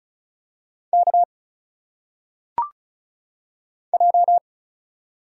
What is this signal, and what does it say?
Morse code, heard as a keyed tone, about 36 wpm, 700 Hz. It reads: K J
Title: Single Letters - Phonetic Pronunciation 35wpm